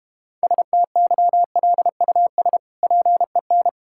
Morse code, heard as a keyed tone, about 32 wpm, 700 Hz. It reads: STYLUS PEN